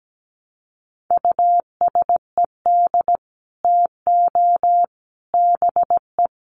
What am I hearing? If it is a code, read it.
Morse code, heard as a keyed tone, about 17 wpm, 700 Hz. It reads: USED TO BE
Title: Sets of 3 Words 17wpm